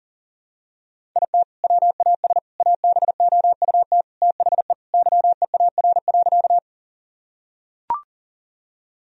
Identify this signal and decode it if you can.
Morse code, heard as a keyed tone, about 40 wpm, 700 Hz. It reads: IT WAS ABOUT THE YEAR.
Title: Sentences from Top 1000 Words - Review Part 1 - 40wpm